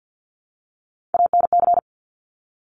Morse code, heard as a keyed tone, about 50 wpm, 700 Hz. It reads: ANC
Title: Top Three Letter Combinations in Words 50wpm